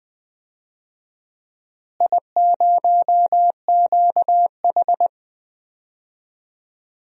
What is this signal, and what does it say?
Morse code, heard as a keyed tone, about 20 wpm, 700 Hz. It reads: I0QH
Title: DXCC-100 Call Signs - 4 Characters - Repeated 3x - 20wpm